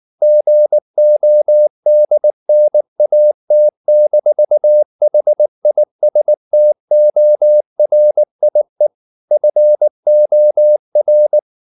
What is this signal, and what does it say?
Morse code, heard as a keyed tone, about 19 wpm, 600 Hz. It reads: GODNAT-HISTORIE FOR
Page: Ferie 19 wpm